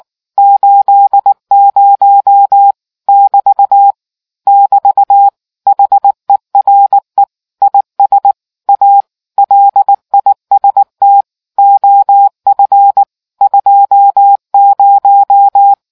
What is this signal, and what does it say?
Morse code, heard as a keyed tone, about 19 wpm, 800 Hz. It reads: 80 <BT> <BT> HERE IS A LIST OF 20